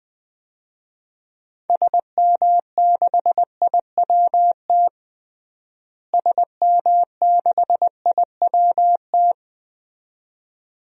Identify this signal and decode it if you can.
Morse code, heard as a keyed tone, about 21 wpm, 700 Hz. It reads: SM6IWT SM6IWT
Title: DXCC-100 Call Signs - 6 Characters - Repeated 2x - 20wpm